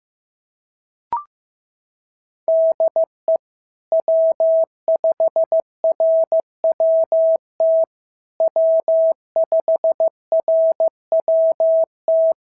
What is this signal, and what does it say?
Morse code, heard as a keyed tone, about 14 wpm, 650 Hz. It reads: DE W5RWT W5RWT
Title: SOTA QSO Practice 15wpm